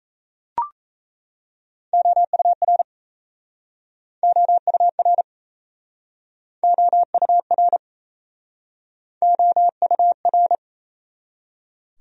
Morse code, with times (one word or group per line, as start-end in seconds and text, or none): 1.9-2.8 OUR
4.2-5.2 OUR
6.6-7.8 OUR
9.2-10.6 OUR